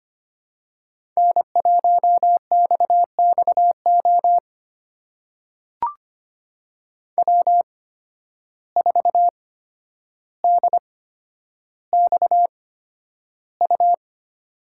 Morse code, heard as a keyed tone, about 25 wpm, spaced as for 11 wpm, 700 Hz. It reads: N1XXO W4DXU
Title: USA Call Signs - 5 Characters - 8x Character Spacing - 25wpm